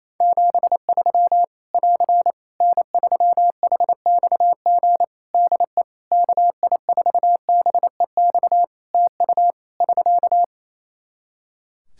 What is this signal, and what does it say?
Morse code, heard as a keyed tone, about 28 wpm, 700 Hz. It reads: 73 <AR> N3HXG DE KI4BEX TU <SK>
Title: K1USN SST QSO Practice 28wpm